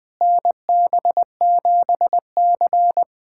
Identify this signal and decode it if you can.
Morse code, heard as a keyed tone, about 20 wpm, 700 Hz. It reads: NB7C
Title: USA Callsign DE Callsign - 4 Characters - 20wpm